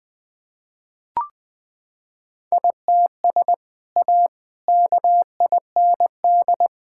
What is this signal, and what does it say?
Morse code, heard as a keyed tone, about 20 wpm, 700 Hz. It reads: ITS A KIND
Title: Sentences with 3-Word Phrases - 20wpm